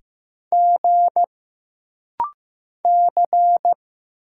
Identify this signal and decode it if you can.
Morse code, heard as a keyed tone, about 15 wpm, 700 Hz. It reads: G C
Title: Single Letters - Rapid-Fire 15wpm